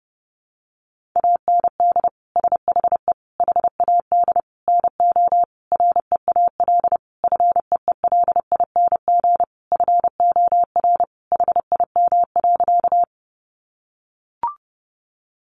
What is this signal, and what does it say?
Morse code, heard as a keyed tone, about 30 wpm, 700 Hz. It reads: AND SHE HAD NO REAL FEELING FOR HIM.
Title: Sentences from Top 1200 Words 30wpm